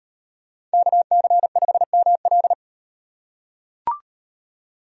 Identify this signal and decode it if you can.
Morse code, heard as a keyed tone, about 38 wpm, 700 Hz. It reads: KC5ML